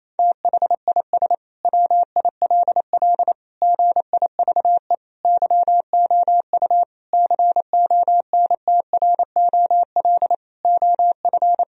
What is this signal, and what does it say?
Morse code, heard as a keyed tone, about 27 wpm, 700 Hz. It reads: THIS WILL GIVE YOU CONTROL OF
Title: Sentences from Top 1000 Words - Review Part 3 - 28wpm